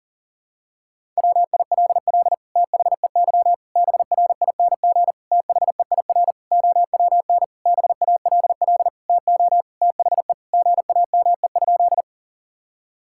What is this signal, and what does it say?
Morse code, heard as a keyed tone, about 40 wpm, 700 Hz. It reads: WILL THEY BRING THEIR OWN BALL TO THE GAME?
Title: Sentences from Top 1000 Words - Review Part 6 - 40wpm